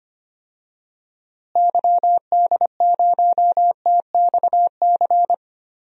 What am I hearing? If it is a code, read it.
Morse code, heard as a keyed tone, about 25 wpm, 700 Hz. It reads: YD0TXC
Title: DXCC-100 Call Signs - 6 Characters - Repeated 3x - 25wpm